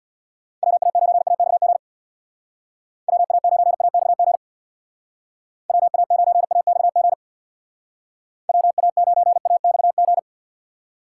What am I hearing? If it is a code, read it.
Morse code, heard as a keyed tone, about 65 wpm, 700 Hz. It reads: WA9AXG WA9AXG WA9AXG WA9AXG